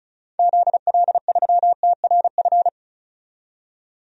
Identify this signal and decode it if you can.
Morse code, heard as a keyed tone, about 34 wpm, 700 Hz. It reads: ZL3TRF
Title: DXCC-100 Call Signs - 6 Characters - 35wpm